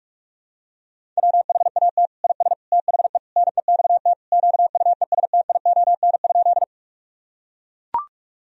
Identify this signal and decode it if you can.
Morse code, heard as a keyed tone, about 42 wpm, 700 Hz. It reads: WHAT IS THE NEXT QUESTION?
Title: Sentences from Top 1000 Words - Review Part 1 - 45wpm